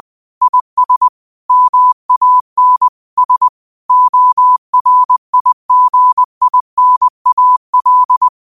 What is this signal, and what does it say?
Morse code, heard as a keyed tone, about 20 wpm, 1000 Hz. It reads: IS MAN S ORIGINAL
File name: qotdfiles / 20wpm